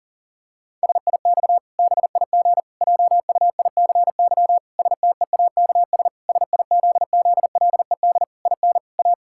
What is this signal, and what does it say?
Morse code, heard as a keyed tone, about 40 wpm, 700 Hz. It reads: SIX BIG JUICY STEAKS SIZZLED IN A
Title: Pangram Challenge v2 - 40wpm